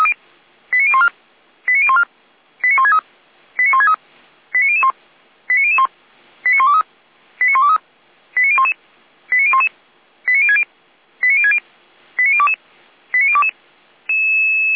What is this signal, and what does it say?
Five-tone selective call (ZVEI), heard as quick runs of five tones, tones of about 70 ms. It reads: 89014 89014 89163 89163 89001 89001 89123 89123 89011 89011 89060 89060 89022 89022
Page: /455/SDR/ZVEI/